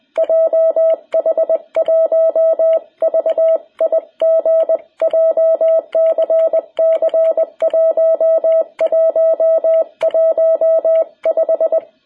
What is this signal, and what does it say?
Morse code, heard as a keyed tone, about 20 wpm, 660 Hz. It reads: JH1VIG JCC 1115
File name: jh8cbh / 2021cw